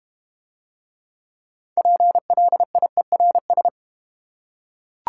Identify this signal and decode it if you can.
Morse code, heard as a keyed tone, about 30 wpm, 690 Hz. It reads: PLIERS
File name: Easy Words - Speed-Racing with Hints - 25wpm